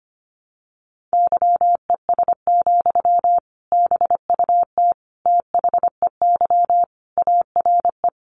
Morse code, heard as a keyed tone, about 24 wpm, 700 Hz. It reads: YES, BUT THEY ARE
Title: Sentences from Top 600 Words 25wpm